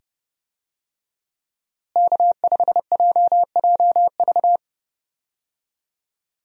Morse code, heard as a keyed tone, about 30 wpm, 700 Hz. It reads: K5JJV
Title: USA Call Signs - 5 Characters - Repeated 2x - 30wpm